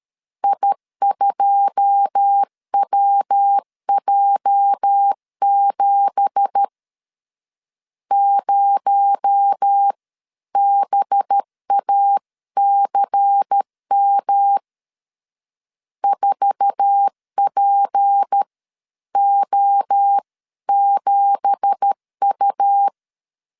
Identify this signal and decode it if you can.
Morse code, heard as a keyed tone, about 13 wpm, spaced as for 11 wpm, 800 Hz. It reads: I2WJ7 0BACM 4PO7U